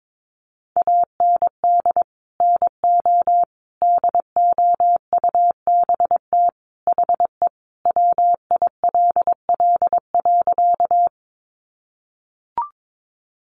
Morse code, heard as a keyed tone, about 22 wpm, 700 Hz. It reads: AND NO DOUBT HE WILL.
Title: Sentences from Top 1500 Words 22wpm